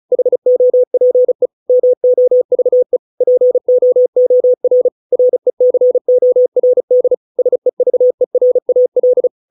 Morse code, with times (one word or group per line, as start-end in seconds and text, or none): 0.1-1.5 HOPE
1.7-3.0 MOVE
3.2-4.9 POOR
5.1-7.1 RECORD
7.4-9.3 SEVERAL